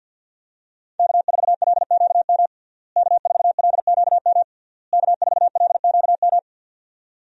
Morse code, heard as a keyed tone, about 45 wpm, 700 Hz. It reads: K4LQM K4LQM K4LQM
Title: USA Call Signs - 5 Characters - Repeated 3x - 50wpm